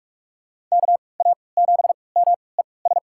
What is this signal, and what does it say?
Morse code, heard as a keyed tone, about 46 wpm, spaced as for 26 wpm, 700 Hz. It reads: KA7MES
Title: USA Call Signs - 6 Characters - 3x Character Spacing - 45wpm